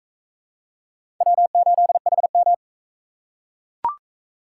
Morse code, heard as a keyed tone, about 41 wpm, 700 Hz. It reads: W8HM